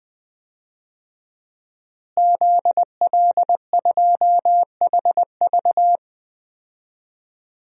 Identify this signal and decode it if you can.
Morse code, heard as a keyed tone, about 20 wpm, 700 Hz. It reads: ZL2HV
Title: DXCC-100 Call Signs - 5 Characters - Repeated 3x - 20wpm